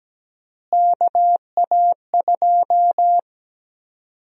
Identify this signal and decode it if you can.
Morse code, heard as a keyed tone, about 17 wpm, 700 Hz. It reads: KA2
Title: USA Call Sign Prefixes - Repeated 2x - 17wpm